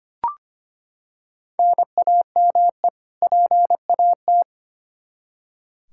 Morse code, heard as a keyed tone, about 25 wpm, 700 Hz. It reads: NAME PAT
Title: Practice QSO - Exchange 1 - Beginning - 25wpm